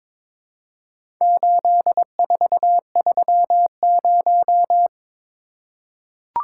Morse code, heard as a keyed tone, about 22 wpm, 700 Hz. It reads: 8430
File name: practice / 22wpm / Four Letter-Number Combinations - 22wpm